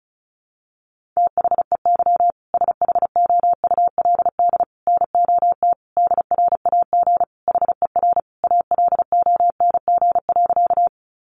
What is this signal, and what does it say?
Morse code, heard as a keyed tone, about 34 wpm, 700 Hz. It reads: THEY SHOULD NOT DRAG HER ALONG.
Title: Sentences from Top 1500 Words 35wpm